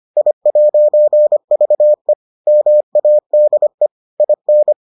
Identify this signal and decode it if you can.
Morse code, heard as a keyed tone, about 25 wpm, 600 Hz. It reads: I'VE MADE IN